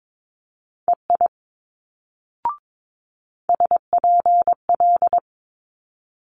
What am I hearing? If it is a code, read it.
Morse code, heard as a keyed tone, about 22 wpm, 700 Hz. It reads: EI SPL
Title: Multi-letter Phonemes Quick Fire 22wpm